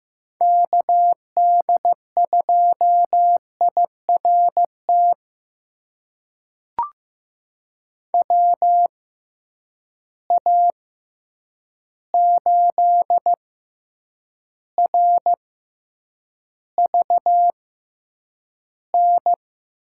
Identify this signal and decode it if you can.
Morse code, heard as a keyed tone, about 15 wpm, spaced as for 8 wpm, 700 Hz. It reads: KD2IRT WA8RVN
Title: USA Call Signs - 6 Characters - 6x Character Spacing - 15wpm